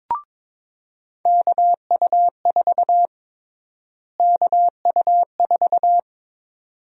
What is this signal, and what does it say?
Morse code, heard as a keyed tone, about 22 wpm, 700 Hz. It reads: KU4 KU4
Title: USA Call Sign Prefixes - Repeated 2x - 22wpm